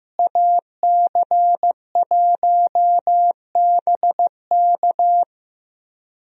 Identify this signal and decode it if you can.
Morse code, heard as a keyed tone, about 15 wpm, 700 Hz. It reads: AC1BK
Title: USA Call Signs - 5 Characters - Repeated 3x - 15wpm